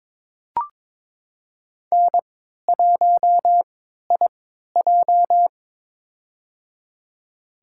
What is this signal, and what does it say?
Morse code, heard as a keyed tone, about 22 wpm, spaced as for 13 wpm, 690 Hz. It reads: N1IJ
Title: USA Call Signs - 4 Characters - 3x Character Spacing - 22wpm